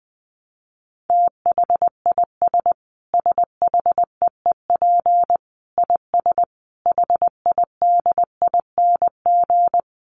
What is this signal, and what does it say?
Morse code, heard as a keyed tone, about 20 wpm, 700 Hz. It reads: THIS SHEEP IS HIDING
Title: Dit Challenge 20wpm